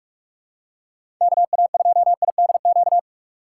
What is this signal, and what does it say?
Morse code, heard as a keyed tone, about 47 wpm, 700 Hz. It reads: KA2IDQ
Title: USA Call Signs - 6 Characters - 6x Character Spacing - 45wpm